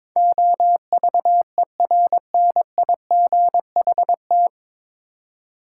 Morse code, heard as a keyed tone, about 22 wpm, 700 Hz. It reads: OVERNIGHT
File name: 9 Letter Words - First 3 Letters Given - 22wpm